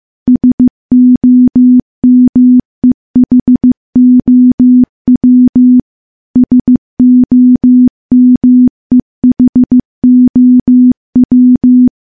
Morse code, heard as a keyed tone, about 15 wpm, 250 Hz. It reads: SOMEHOW SOMEHOW
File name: morse / morse-training-15WPM